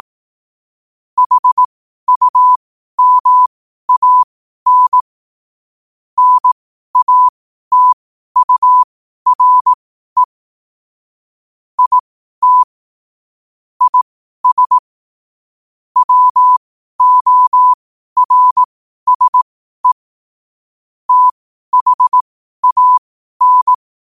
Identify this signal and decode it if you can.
Morse code, heard as a keyed tone, about 18 wpm, spaced as for 12 wpm, 1000 Hz. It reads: HUMAN NATURE IT IS WORSE THAN
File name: qotdfiles / 12wpm